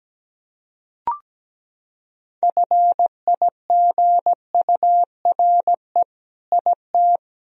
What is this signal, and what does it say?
Morse code, heard as a keyed tone, about 17 wpm, 700 Hz. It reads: FIGURE IT
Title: Sentences from Top 1000 Words - Review Part 2 - 17wpm